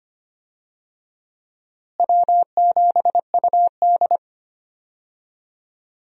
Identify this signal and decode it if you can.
Morse code, heard as a keyed tone, about 25 wpm, 700 Hz. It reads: W7UD
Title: USA Call Signs - 4 Characters - Repeated 2x - 25wpm